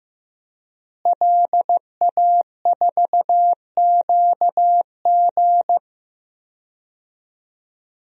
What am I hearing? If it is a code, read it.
Morse code, heard as a keyed tone, about 15 wpm, 700 Hz. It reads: LA4QG
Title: DXCC-100 Call Signs - 5 Characters - Repeated 2x - 15wpm